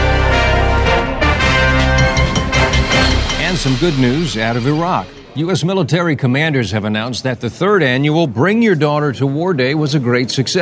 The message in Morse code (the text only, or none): none